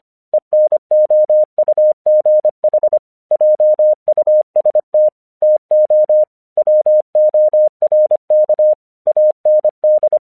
Morse code, heard as a keyed tone, about 25 wpm, 610 Hz. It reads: ENOUGH JUST TO WORK AND
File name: 1 English 25wpm / AQC11